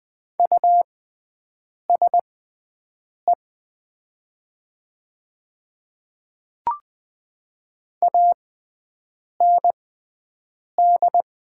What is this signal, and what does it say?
Morse code, heard as a keyed tone, about 20 wpm, spaced as for 7 wpm, 690 Hz. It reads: USE AND
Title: 6x Character Spacing - Top 100 Words - 20wpm